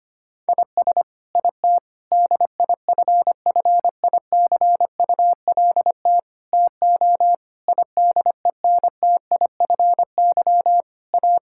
Morse code, heard as a keyed tone, about 25 wpm, 700 Hz. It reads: IS IT DIFFICULT TO IDENTIFY A
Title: Sentences from Top 1000 Words - Review Part 6 - 25wpm